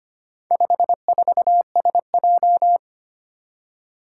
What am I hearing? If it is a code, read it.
Morse code, heard as a keyed tone, about 25 wpm, 700 Hz. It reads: 54SJ
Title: Four Challenging Letter-Number Combinations 25wpm